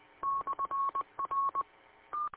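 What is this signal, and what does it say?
Morse code, heard as a keyed tone, about 20 wpm, 1100 Hz. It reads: /R
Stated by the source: Morse code received at station W8UM.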